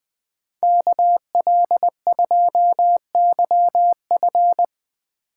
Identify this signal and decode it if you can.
Morse code, heard as a keyed tone, about 23 wpm, 700 Hz. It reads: KL2YF
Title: Speed Racing - USA 2x2 Call Signs 20wpm